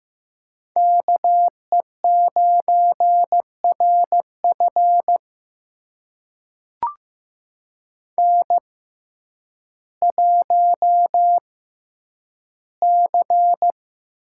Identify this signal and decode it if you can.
Morse code, heard as a keyed tone, about 15 wpm, spaced as for 9 wpm, 700 Hz. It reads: KE9RF N1C
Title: USA Call Signs - 6x Character Spacing - 15wpm